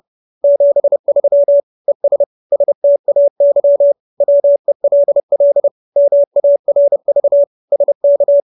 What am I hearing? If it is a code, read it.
Morse code, heard as a keyed tone, about 30 wpm, 570 Hz. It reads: 73 ES STAY WELL MARV SK